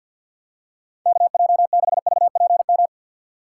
Morse code, heard as a keyed tone, about 50 wpm, 700 Hz. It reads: KJ6VPM
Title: USA Call Signs - 6 Characters - 6x Character Spacing - 50wpm